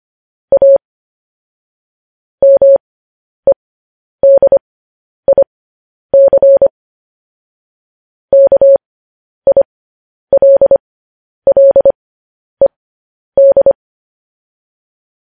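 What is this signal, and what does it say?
Morse code, heard as a keyed tone, about 25 wpm, spaced as for 10 wpm, 550 Hz. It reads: A MEDIC KILLED
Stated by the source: Morse code news headlines at 10 wpm.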